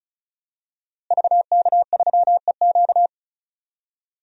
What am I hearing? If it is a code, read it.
Morse code, heard as a keyed tone, about 34 wpm, 700 Hz. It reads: VK3EQ
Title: DXCC-100 Call Signs - 5 Characters - Repeated 3x - 35wpm